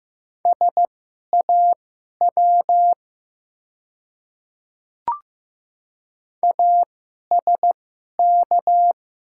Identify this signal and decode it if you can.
Morse code, heard as a keyed tone, about 15 wpm, spaced as for 11 wpm, 700 Hz. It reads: SAW ASK
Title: 2x Character Spacing - Top 500 Words - 15wpm